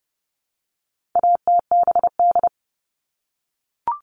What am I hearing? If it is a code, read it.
Morse code, heard as a keyed tone, about 30 wpm, 700 Hz. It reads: ATBD
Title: All Cut Numbers - 4-Digit 30wpm